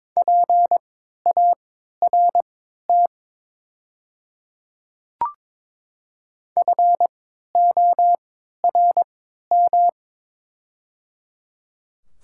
Morse code, heard as a keyed tone, about 22 wpm, spaced as for 12 wpm, 700 Hz. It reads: PART FORM
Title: 3x Character Spacing - Top 200 Words - 22wpm